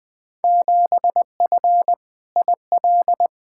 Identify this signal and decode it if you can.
Morse code, heard as a keyed tone, about 20 wpm, 700 Hz. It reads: 7F IL
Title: Field Day Exchanges Encore - 20wpm